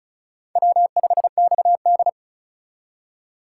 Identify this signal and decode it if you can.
Morse code, heard as a keyed tone, about 45 wpm, 700 Hz. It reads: W5XD